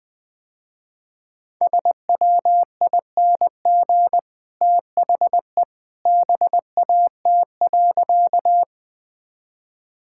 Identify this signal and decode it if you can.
Morse code, heard as a keyed tone, about 21 wpm, 700 Hz. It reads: SWING THE BAT.
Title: Easy Sentences with Hints - 20wpm